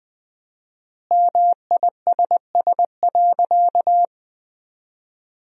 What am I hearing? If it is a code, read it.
Morse code, heard as a keyed tone, about 21 wpm, 700 Hz. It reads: MISS.
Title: 4x Word-Spacing - Sentences from Top 500 Words - 20wpm